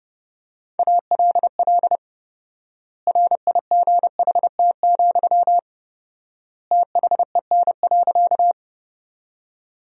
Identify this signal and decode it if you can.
Morse code, heard as a keyed tone, about 32 wpm, 700 Hz. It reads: ALL RIGHT, THEN.
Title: 4x Word-Spacing - Sentences from Top 200 Words - 30wpm